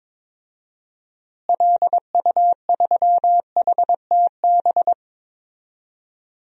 Morse code, heard as a keyed tone, about 22 wpm, 700 Hz. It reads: LU3HTB
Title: DXCC-100 Call Signs - 6 Characters - Repeated 3x - 22wpm